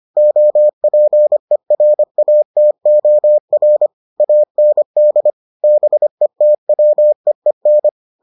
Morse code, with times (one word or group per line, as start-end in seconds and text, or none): 0.2-3.9 OPERATOR
4.2-5.3 AND
5.6-7.9 BETWEEN